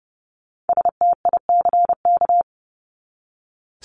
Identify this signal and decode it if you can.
Morse code, heard as a keyed tone, about 30 wpm, 700 Hz. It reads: STICK
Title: Top 1001-1250 Words 30wpm